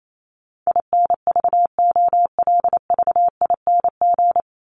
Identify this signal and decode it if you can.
Morse code, heard as a keyed tone, about 28 wpm, 700 Hz. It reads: INVOLVING
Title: ING Suffix - Top 100 Words with Base Words 28wpm